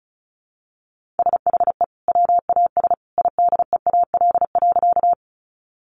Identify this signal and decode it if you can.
Morse code, heard as a keyed tone, about 34 wpm, 700 Hz. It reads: SHE WAS IDEAL.
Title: Sentences from Top 1400 Words 35wpm